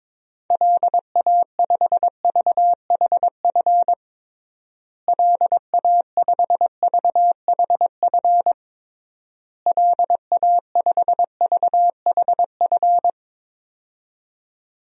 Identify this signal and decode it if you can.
Morse code, heard as a keyed tone, about 22 wpm, 700 Hz. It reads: LA5VHF LA5VHF LA5VHF
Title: DXCC-100 Call Signs - 6 Characters - Repeated 3x - 22wpm